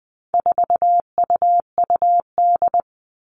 Morse code, heard as a keyed tone, about 20 wpm, 700 Hz. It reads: 4UUD
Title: All Cut Numbers - 4-Digit 20wpm